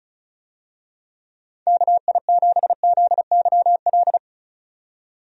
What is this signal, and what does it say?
Morse code, heard as a keyed tone, about 36 wpm, 700 Hz. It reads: KI7ZYL
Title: USA Call Signs - 6 Characters - 35wpm